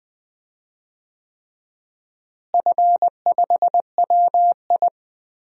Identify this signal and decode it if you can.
Morse code, heard as a keyed tone, about 21 wpm, 700 Hz. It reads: F5WI
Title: DXCC-100 Call Signs - 4 Characters - Repeated 3x - 20wpm